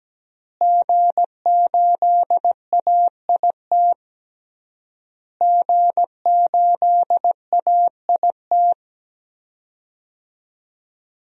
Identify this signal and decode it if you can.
Morse code, heard as a keyed tone, about 17 wpm, 700 Hz. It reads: G8AIT G8AIT
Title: DXCC-100 Call Signs - 5 Characters - Repeated 3x - 17wpm